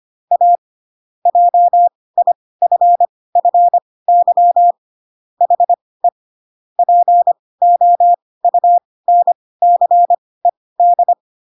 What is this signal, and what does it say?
Morse code, heard as a keyed tone, about 25 wpm, spaced as for 18 wpm, 700 Hz. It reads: A JIFFY HE POUNCED
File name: Pinocchio / Pinocchio ch5 12 25 ik1hge